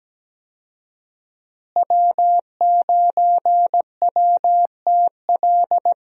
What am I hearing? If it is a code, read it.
Morse code, heard as a keyed tone, about 17 wpm, 700 Hz. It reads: W9WTL
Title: USA Call Signs - 5 Characters - Repeated 2x - 17wpm